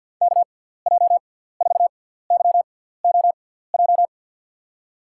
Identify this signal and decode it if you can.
Morse code, heard as a keyed tone, about 52 wpm, spaced as for 17 wpm, 700 Hz. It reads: KJ4YOJ
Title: USA Call Signs - 6 Characters - 6x Character Spacing - 50wpm